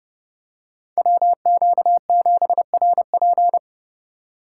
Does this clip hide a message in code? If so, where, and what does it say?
Yes: Morse code from 1.0 to 3.6 s, heard as a keyed tone, about 31 wpm, 700 Hz. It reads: WQ7RP